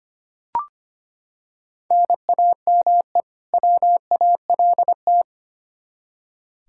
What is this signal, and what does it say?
Morse code, heard as a keyed tone, about 25 wpm, 700 Hz. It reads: NAME WALT
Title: Practice QSO - Exchange 1 - Beginning - 25wpm